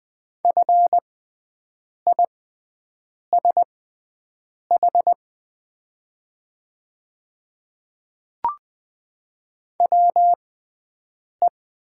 Morse code, heard as a keyed tone, about 20 wpm, spaced as for 7 wpm, 700 Hz. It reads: FISH WE
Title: 6x Character Spacing - Top 300 Words - 20wpm